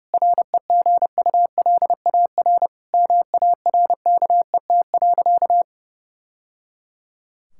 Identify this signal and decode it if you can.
Morse code, heard as a keyed tone, about 32 wpm, 700 Hz. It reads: REGULAR MARKET.
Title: Sentences from Top 1000 Words - Review Part 6 - 30wpm